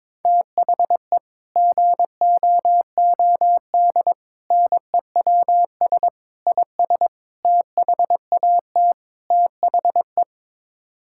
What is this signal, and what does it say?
Morse code, heard as a keyed tone, about 22 wpm, 700 Hz. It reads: THE GOOD NEWS IS THAT THE
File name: practice / 22wpm / Sets of 6 Words - 22wpm